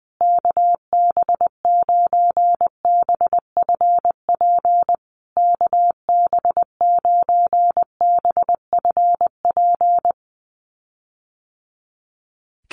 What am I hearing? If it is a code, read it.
Morse code, heard as a keyed tone, about 20 wpm, 700 Hz. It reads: KB9BFP KB9BFP
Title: USA Call Signs Repeated 3x 20wpm